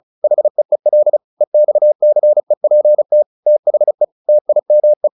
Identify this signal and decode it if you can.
Morse code, heard as a keyed tone, about 34 wpm, 600 Hz. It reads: HEEL EXCEPT THE TIME
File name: the old man and the sea 2 35wpm